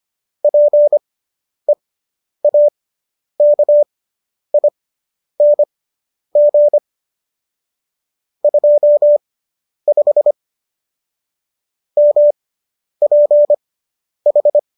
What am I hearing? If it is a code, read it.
Morse code, heard as a keyed tone, about 25 wpm, spaced as for 10 wpm, 590 Hz. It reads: PEAKING 25 MPH